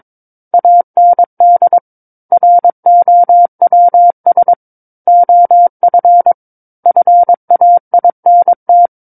Morse code, heard as a keyed tone, about 22 wpm, 700 Hz. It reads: AND ROWS OF FAINT
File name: Book One / 13-Chapter 13 - 22 WPM